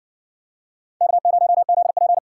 Morse code, heard as a keyed tone, about 53 wpm, 700 Hz. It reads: K0ZP